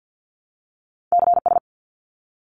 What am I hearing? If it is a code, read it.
Morse code, heard as a keyed tone, about 50 wpm, 700 Hz. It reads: CS
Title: Abbreviations 50wpm